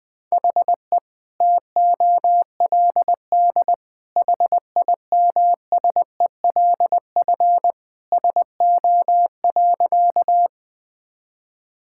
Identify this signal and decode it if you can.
Morse code, heard as a keyed tone, about 20 wpm, 700 Hz. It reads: HE TOLD HIMSELF SO.
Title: Sentences from Top 1600 Words 20wpm